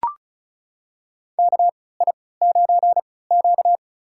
Morse code, heard as a keyed tone, about 34 wpm, spaced as for 20 wpm, 700 Hz. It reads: KI9Q
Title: USA Call Signs - 3x Character Spacing - 35wpm